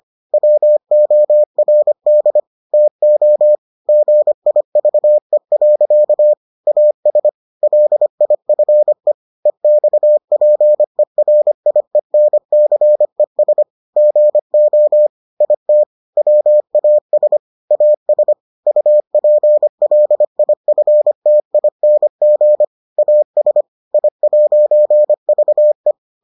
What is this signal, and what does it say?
Morse code, heard as a keyed tone, about 25 wpm, 600 Hz. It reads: WORD TO GIVE. AS LIFE EXPERIENCES GO IT WAS AS UPLIFTING AS I'VE